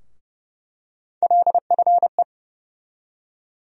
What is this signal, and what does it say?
Morse code, heard as a keyed tone, about 31 wpm, 700 Hz. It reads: LFE
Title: USA Call Sign Postfixes - 3 Letters 30wpm